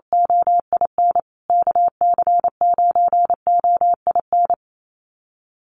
Morse code, heard as a keyed tone, about 28 wpm, 700 Hz. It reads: OIN KC9OIN
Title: USA Call Signs Repeated 2x 28wpm